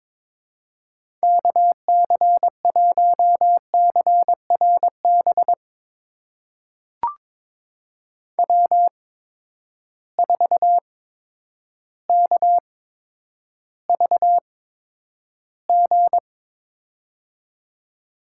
Morse code, heard as a keyed tone, about 22 wpm, spaced as for 10 wpm, 700 Hz. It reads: KC1CRB W4KVG
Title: USA Call Signs - 8x Character Spacing - 22wpm